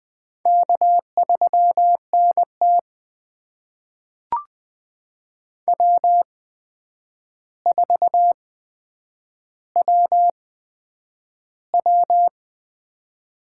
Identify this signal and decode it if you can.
Morse code, heard as a keyed tone, about 20 wpm, spaced as for 9 wpm, 700 Hz. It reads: K3NT W4WW
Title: USA Call Signs - 4 Characters - 8x Character Spacing - 20wpm